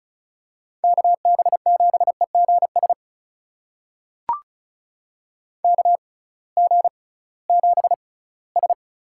Morse code, heard as a keyed tone, about 36 wpm, spaced as for 20 wpm, 700 Hz. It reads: KB7EGS KG7S
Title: USA Call Signs - 6x Character Spacing - 35wpm